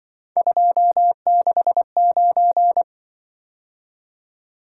Morse code, heard as a keyed tone, about 25 wpm, 700 Hz. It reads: 269
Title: Speed-Racing - Three Letter-Number Combinations - 20wpm